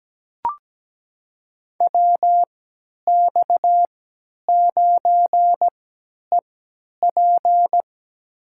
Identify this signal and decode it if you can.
Morse code, heard as a keyed tone, about 17 wpm, spaced as for 10 wpm, 700 Hz. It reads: WX9EP
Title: USA Call Signs - 5 Characters - 3x Character Spacing - 17wpm